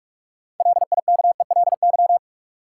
Morse code, heard as a keyed tone, about 44 wpm, 700 Hz. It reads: LIKELY
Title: Sets of 2 Words - New Word Drill - 45wpm